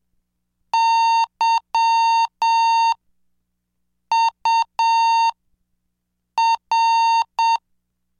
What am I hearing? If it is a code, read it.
Morse code, heard as a keyed tone, about 7 wpm, spaced as for 5 wpm, 910 Hz. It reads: YUR
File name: cursos / CDestudo / 20 Track 20